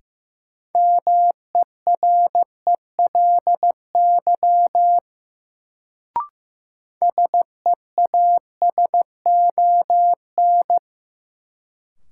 Morse code, heard as a keyed tone, about 15 wpm, 700 Hz. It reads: MERELY SEASON
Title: Sentences from Top 1400 Words - New Word Drill Alt - 15wpm